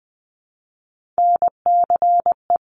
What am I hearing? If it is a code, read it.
Morse code, heard as a keyed tone, about 20 wpm, 700 Hz. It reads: NCE
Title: Top Three Letter Combinations in Words 20wpm